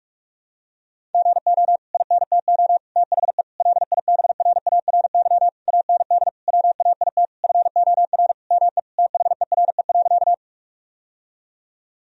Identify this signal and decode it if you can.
Morse code, heard as a keyed tone, about 46 wpm, 700 Hz. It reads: GO INTO THE LIBRARY AND WAIT FOR ME THERE.
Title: Sentences from Top 1000 Words - Review Part 4 - 45wpm